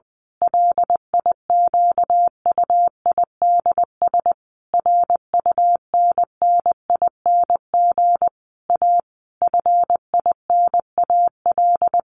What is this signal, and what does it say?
Morse code, heard as a keyed tone, about 20 wpm, 700 Hz. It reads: LIQUIDS RUNNING A FINAL
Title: A Sound Of Thunder 20wpm